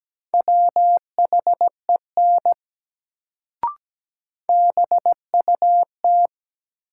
Morse code, heard as a keyed tone, about 17 wpm, 700 Hz. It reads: WHEN BUT
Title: Sentences from Top 100 Words - New Word Drill - 17wpm